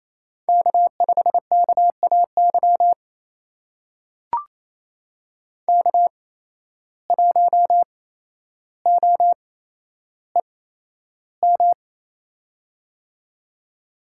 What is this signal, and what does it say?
Morse code, heard as a keyed tone, about 28 wpm, spaced as for 12 wpm, 700 Hz. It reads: K5KAY K1OEM
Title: USA Call Signs - 5 Characters - 8x Character Spacing - 28wpm